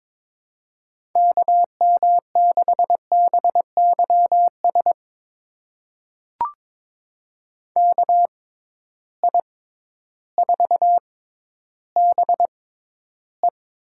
Morse code, heard as a keyed tone, about 21 wpm, spaced as for 12 wpm, 700 Hz. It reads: KM6BYS KI4BE
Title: USA Call Signs - 6 Characters - 6x Character Spacing - 22wpm